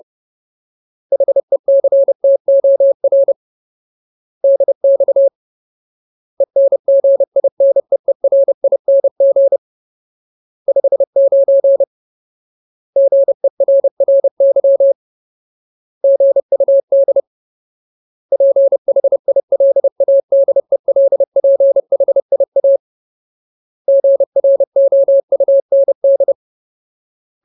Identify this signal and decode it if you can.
Morse code, heard as a keyed tone, about 30 wpm, 550 Hz. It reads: HECTOR DX ENGINEERING 59 GERRY GUD PHILADELPHIA GROUND
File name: Ultimate Word List / 30WPM Ultimate